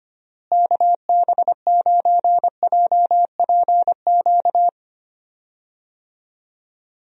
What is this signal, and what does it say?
Morse code, heard as a keyed tone, about 25 wpm, 700 Hz. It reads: KB9JPQ
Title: USA Call Signs - 6 Characters - Repeated 3x - 25wpm